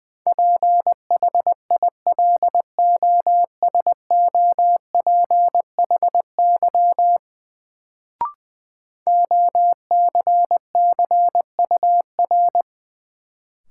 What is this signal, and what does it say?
Morse code, heard as a keyed tone, about 20 wpm, 700 Hz. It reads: PHILOSOPHY OCCUR
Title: Sentences from Top 900 Words - New Word Drill - 20wpm